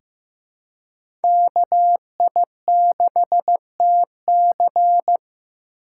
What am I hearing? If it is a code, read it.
Morse code, heard as a keyed tone, about 15 wpm, 700 Hz. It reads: KI6TC
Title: USA Call Signs - 2x Character Spacing - 15wpm